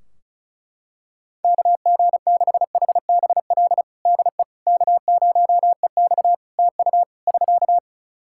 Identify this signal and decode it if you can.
Morse code, heard as a keyed tone, about 36 wpm, 700 Hz. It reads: KG6HBL DE K0EX TU <SK>
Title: K1USN SST QSO Practice 35wpm